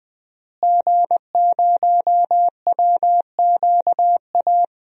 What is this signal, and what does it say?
Morse code, heard as a keyed tone, about 20 wpm, 700 Hz. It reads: G0WQA
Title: DXCC-100 Call Signs - 5 Characters - Repeated 2x - 20wpm